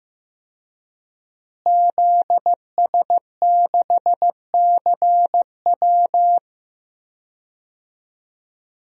Morse code, heard as a keyed tone, about 15 wpm, 700 Hz. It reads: ZS6CW
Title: DXCC-100 Call Signs - 5 Characters - Repeated 2x - 15wpm